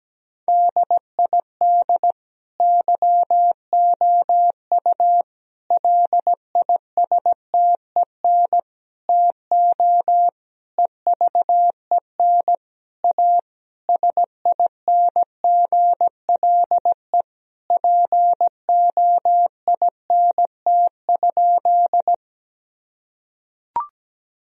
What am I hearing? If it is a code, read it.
Morse code, heard as a keyed tone, about 17 wpm, 700 Hz. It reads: DID YOU LISTEN TO EVEN A SINGLE POINT?
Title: Sentences from Top 1000 Words - Review Part 4 - 17wpm